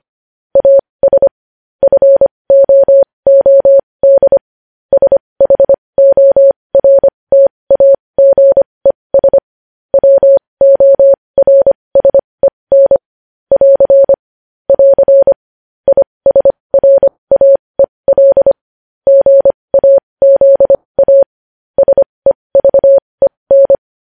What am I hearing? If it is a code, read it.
Morse code, heard as a keyed tone, about 25 wpm, spaced as for 20 wpm, 550 Hz. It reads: AS FOOD SHORTAGES WORSEN <AR> <AR> ISRAEL GAZA SEVEN
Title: News Headlines at 20